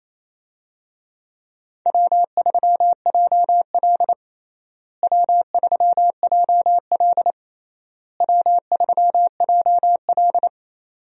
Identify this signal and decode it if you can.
Morse code, heard as a keyed tone, about 27 wpm, 700 Hz. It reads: W3JL W3JL W3JL
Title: USA Call Signs - 4 Characters - Repeated 3x - 28wpm